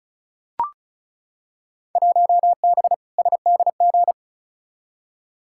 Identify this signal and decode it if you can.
Morse code, heard as a keyed tone, about 36 wpm, 690 Hz. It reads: 1B SDG
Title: Field Day Exchanges Encore - 35wpm